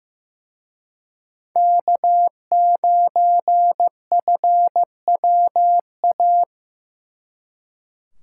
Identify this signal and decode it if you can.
Morse code, heard as a keyed tone, about 18 wpm, 700 Hz. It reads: K9FWA
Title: Speed Racing - USA 1x3 Call Signs 15wpm